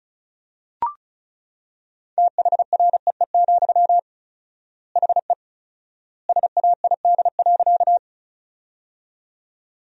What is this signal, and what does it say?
Morse code, heard as a keyed tone, about 36 wpm, 700 Hz. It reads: THREE, HE SAID.
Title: 4x Word-Spacing - Sentences from Top 200 Words - 35wpm